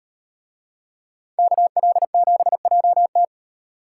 Morse code, heard as a keyed tone, about 36 wpm, 700 Hz. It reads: KL7JT